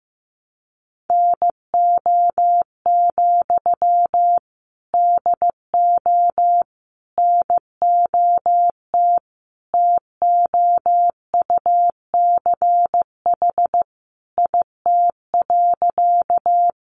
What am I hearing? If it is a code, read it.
Morse code, heard as a keyed tone, about 15 wpm, 700 Hz. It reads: NO, DO NOT TOUCH IT.
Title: Sentences from Top 1200 Words 15wpm